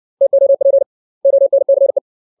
Morse code, heard as a keyed tone, about 60 wpm, 550 Hz. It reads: TOP ONCE
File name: 632 words / 632 Words 60WPM